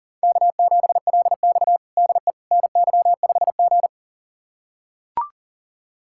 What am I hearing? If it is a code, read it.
Morse code, heard as a keyed tone, about 40 wpm, 700 Hz. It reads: K7LX DE NY5G
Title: USA Callsign DE Callsign - 4 Characters - 40wpm